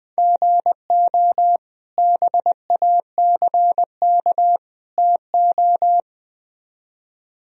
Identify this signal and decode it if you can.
Morse code, heard as a keyed tone, about 20 wpm, 700 Hz. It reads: GO BACK TO
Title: Speed-Racing - Sets of 3 Words 17wpm